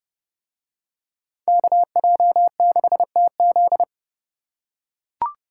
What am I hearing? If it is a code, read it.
Morse code, heard as a keyed tone, about 28 wpm, 700 Hz. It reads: KJ6TZ